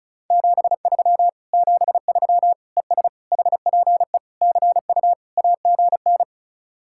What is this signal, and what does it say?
Morse code, heard as a keyed tone, about 40 wpm, 700 Hz. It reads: 73 73 ES HPE CU AGN